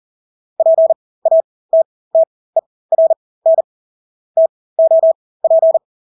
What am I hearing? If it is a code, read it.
Morse code, heard as a keyed tone, about 40 wpm, spaced as for 20 wpm, 650 Hz. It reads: PATTERN TOP
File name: Engelske ord 40-20 wpm